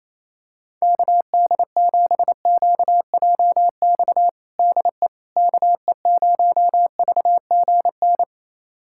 Morse code, heard as a keyed tone, about 28 wpm, 700 Hz. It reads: KD7QJX DE KE0VGN
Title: USA Callsign DE Callsign 28wpm